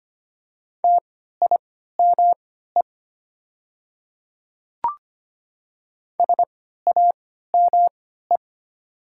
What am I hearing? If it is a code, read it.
Morse code, heard as a keyed tone, about 25 wpm, spaced as for 14 wpm, 700 Hz. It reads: TIME SAME
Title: 3x Character Spacing - Top 200 Words - 25wpm